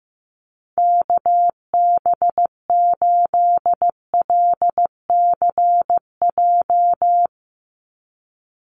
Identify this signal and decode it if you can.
Morse code, heard as a keyed tone, about 15 wpm, 700 Hz. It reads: KB8LCJ
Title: USA Call Signs 15wpm